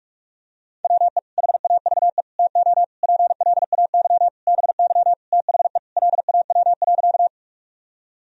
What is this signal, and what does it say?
Morse code, heard as a keyed tone, about 46 wpm, 700 Hz. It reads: WE HAVE TO PLAY BY THE LAW.
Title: Sentences from Top 1000 Words - Review Part 3 - 45wpm